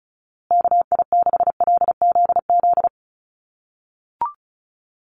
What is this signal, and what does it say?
Morse code, heard as a keyed tone, about 34 wpm, 700 Hz. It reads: KI6LZZ
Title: USA Call Signs Repeated 3x 35wpm